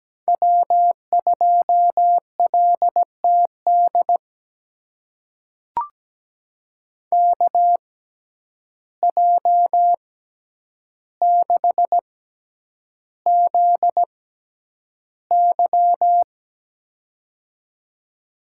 Morse code, heard as a keyed tone, about 17 wpm, spaced as for 9 wpm, 700 Hz. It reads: W2LTD KJ6ZY
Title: USA Call Signs - 5 Characters - 6x Character Spacing - 17wpm